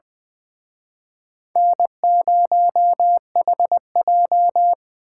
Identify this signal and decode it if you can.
Morse code, heard as a keyed tone, about 25 wpm, 700 Hz. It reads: N0HJ
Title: Speed Racing - USA 1x2 Call Signs 20wpm